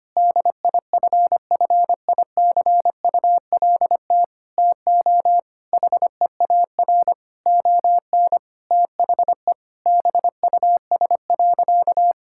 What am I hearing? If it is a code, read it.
Morse code, heard as a keyed tone, about 25 wpm, 700 Hz. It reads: DIFFICULT TO HEAR ON THE BUS.
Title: Sentences from Top 1000 Words - Review Part 6 - 25wpm